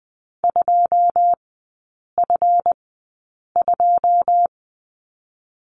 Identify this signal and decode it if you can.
Morse code, heard as a keyed tone, about 20 wpm, spaced as for 8 wpm, 700 Hz. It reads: 2F2
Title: 152 Lesson 25e - Letter p - Three Characters 20wpm